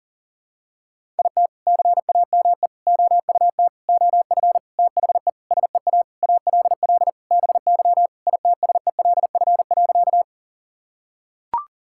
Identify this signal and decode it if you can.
Morse code, heard as a keyed tone, about 40 wpm, 700 Hz. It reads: IT CAME OUT OF THE SEA ALL BY ITSELF.
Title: Sentences from Top 1000 Words - Review Part 5 - 40wpm